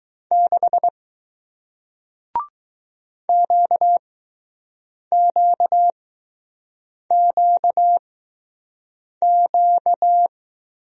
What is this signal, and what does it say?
Morse code, heard as a keyed tone, about 20 wpm, spaced as for 6 wpm, 700 Hz. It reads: 6 QQQQ